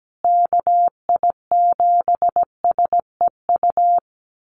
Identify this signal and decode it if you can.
Morse code, heard as a keyed tone, about 17 wpm, 700 Hz. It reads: KI7SEU
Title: USA Call Signs Repeated 3x 17wpm